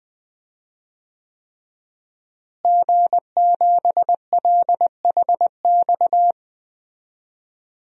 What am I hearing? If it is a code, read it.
Morse code, heard as a keyed tone, about 20 wpm, 700 Hz. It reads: G7LHX
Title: DXCC-100 Call Signs - 5 Characters - Repeated 3x - 20wpm